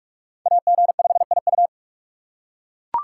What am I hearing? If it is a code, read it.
Morse code, heard as a keyed tone, about 42 wpm, 700 Hz. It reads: AG5IU
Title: USA Call Signs - 5 Characters - 4x Character Spacing - 45wpm